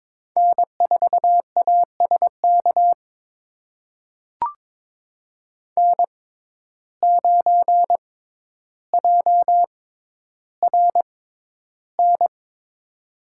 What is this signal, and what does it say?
Morse code, heard as a keyed tone, about 22 wpm, spaced as for 11 wpm, 700 Hz. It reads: N4ASK N9JRN
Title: USA Call Signs - 5 Characters - 6x Character Spacing - 22wpm